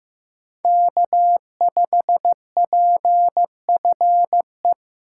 Morse code, heard as a keyed tone, about 18 wpm, 700 Hz. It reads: K5PFE
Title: Speed Racing - USA 1x3 Call Signs 15wpm